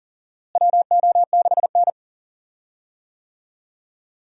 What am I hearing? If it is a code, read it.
Morse code, heard as a keyed tone, about 40 wpm, 700 Hz. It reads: WO6N
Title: USA Call Signs - 4 Characters - Repeated 2x - 40wpm